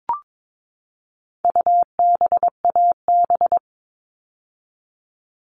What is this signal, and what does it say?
Morse code, heard as a keyed tone, about 22 wpm, 700 Hz. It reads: UBAB
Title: All Cut Numbers - 4-Digit 22wpm